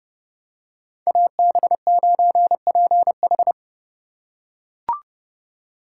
Morse code, heard as a keyed tone, about 27 wpm, 700 Hz. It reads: AB9PH